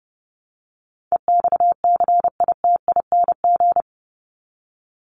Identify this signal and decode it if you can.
Morse code, heard as a keyed tone, about 30 wpm, 700 Hz. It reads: EXCITING
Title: Top 1751-2000 Words 30wpm